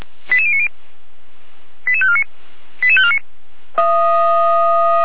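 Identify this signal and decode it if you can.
Five-tone selective call (ZVEI), heard as quick runs of five tones, tones of about 70 ms. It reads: 88090 88549 88549
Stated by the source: Five-tone ZVEI selective call 88012, received on SDR by feuerwehr-orlamuende.org fire brigade server.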